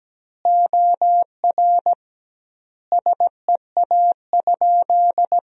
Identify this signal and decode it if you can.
Morse code, heard as a keyed tone, about 17 wpm, 700 Hz. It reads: OR SEA?
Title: 2x Word-Spacing - Sentences from Top 500 Words - 17wpm